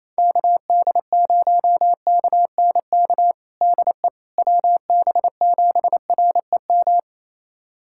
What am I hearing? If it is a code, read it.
Morse code, heard as a keyed tone, about 27 wpm, 700 Hz. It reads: KD0KNK DE WB7REM
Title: USA Callsign DE Callsign - 6 Characters - 28wpm